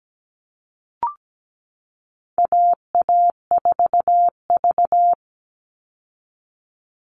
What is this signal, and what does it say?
Morse code, heard as a keyed tone, about 17 wpm, 700 Hz. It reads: AA4V
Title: Call Signs Repeated 2x - 17wpm